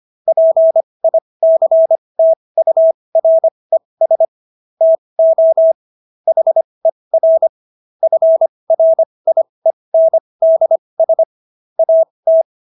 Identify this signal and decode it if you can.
Morse code, heard as a keyed tone, about 25 wpm, spaced as for 20 wpm, 650 Hz. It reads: PICTURES TO HER FRIENDS AT